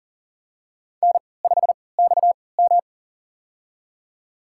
Morse code, heard as a keyed tone, about 40 wpm, spaced as for 23 wpm, 700 Hz. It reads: N5XM
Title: USA Call Signs - 4 Characters - 3x Character Spacing - 40wpm